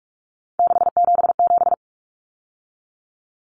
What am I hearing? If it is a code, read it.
Morse code, heard as a keyed tone, about 44 wpm, 700 Hz. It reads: 677
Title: Common Cut Numbers - 3-Digit 45wpm